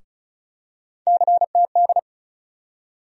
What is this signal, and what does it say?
Morse code, heard as a keyed tone, about 34 wpm, 700 Hz. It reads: CTD
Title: USA Call Sign Postfixes - 3 Letters 35wpm